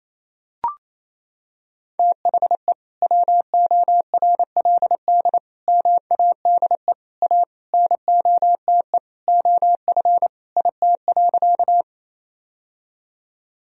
Morse code, heard as a keyed tone, about 28 wpm, 690 Hz. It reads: THE WORLD MADE A NOTE OF IT.
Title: Sentences from Top 1000 Words - Review Part 2 - 28wpm